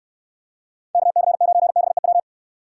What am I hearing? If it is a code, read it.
Morse code, heard as a keyed tone, about 66 wpm, 700 Hz. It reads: KC9BEW